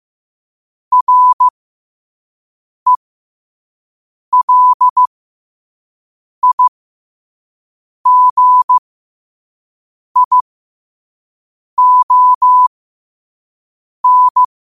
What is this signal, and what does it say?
Morse code, heard as a keyed tone, about 15 wpm, spaced as for 5 wpm, 1000 Hz. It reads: RELIGION
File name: qotdfiles / 5wpm